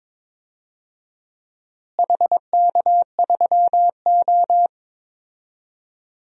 Morse code, heard as a keyed tone, about 22 wpm, 700 Hz. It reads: HK3O
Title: DXCC-100 Call Signs - 4 Characters - Repeated 2x - 22wpm